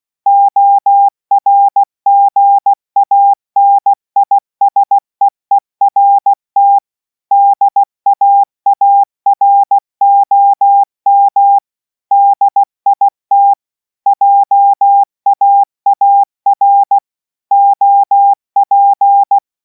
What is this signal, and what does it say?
Morse code, heard as a keyed tone, about 16 wpm, 800 Hz. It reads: ORGANISEERT DAAROM DIT JAAR OP